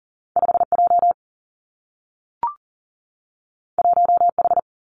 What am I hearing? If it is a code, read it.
Morse code, heard as a keyed tone, about 42 wpm, 700 Hz. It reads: 5 J 1 H
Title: Two Challenging Letter-Number Combinations 40wpm